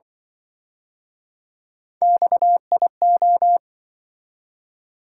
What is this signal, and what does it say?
Morse code, heard as a keyed tone, about 27 wpm, 700 Hz. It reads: XIO